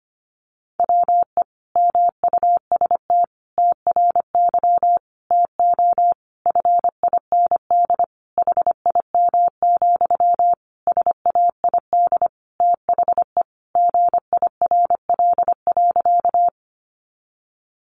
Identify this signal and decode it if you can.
Morse code, heard as a keyed tone, about 25 wpm, 700 Hz. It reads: WE MUST TRY TO FIND HIM, SAID THE GIRL.
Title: Sentences from Top 400 Words 25wpm